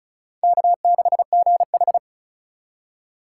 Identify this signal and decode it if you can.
Morse code, heard as a keyed tone, about 36 wpm, 700 Hz. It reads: K6GH
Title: USA Call Signs - 4 Characters - 2x Character Spacing - 35wpm